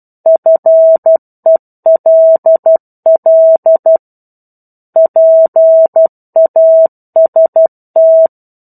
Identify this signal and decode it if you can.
Morse code, heard as a keyed tone, about 12 wpm, 650 Hz. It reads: FELL PAST
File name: 01-Chapter-01 12 WPM